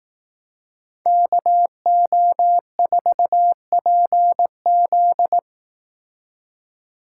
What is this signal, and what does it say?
Morse code, heard as a keyed tone, about 18 wpm, 700 Hz. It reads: KO4PZ
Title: Speed Racing - USA 2x2 Call Signs 15wpm